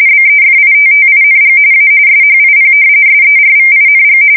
RTTY, about 45.45 baud. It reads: BEFITTING A SHIP SERVING
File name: fsk170 / battleshipnj